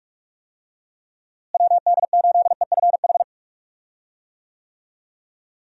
Morse code, heard as a keyed tone, about 47 wpm, 700 Hz. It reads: WD8EFH
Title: USA Call Signs - 6 Characters - Repeated 3x - 45wpm